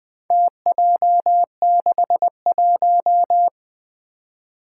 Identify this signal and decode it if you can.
Morse code, heard as a keyed tone, about 20 wpm, 700 Hz. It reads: TJ61
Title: Four Challenging Letter-Number Combinations 20wpm